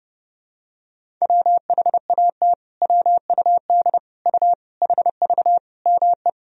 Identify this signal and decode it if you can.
Morse code, heard as a keyed tone, about 32 wpm, 700 Hz. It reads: WHAT WUD U HV ME
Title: Sentences from Top 300 Words and Abbreviations 30wpm